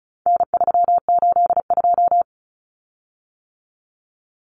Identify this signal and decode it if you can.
Morse code, heard as a keyed tone, about 34 wpm, 700 Hz. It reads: N382
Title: Common Cut Numbers - 4-Digit 35wpm